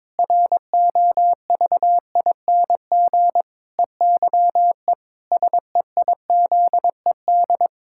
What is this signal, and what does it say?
Morse code, heard as a keyed tone, about 22 wpm, 700 Hz. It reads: ROVING EYE SEIZED
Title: Pangram Challenge v2 - 22wpm